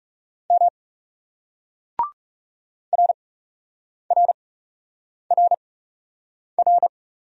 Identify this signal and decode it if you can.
Morse code, heard as a keyed tone, about 37 wpm, spaced as for 8 wpm, 700 Hz. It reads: M RRRR
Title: Speed-Racing Single Letters 30wpm